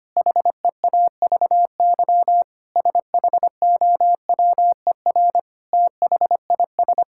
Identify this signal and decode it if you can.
Morse code, heard as a keyed tone, about 25 wpm, 700 Hz. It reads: HEAVY SHOWER THIS
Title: Sentences from Top 1600 Words 25wpm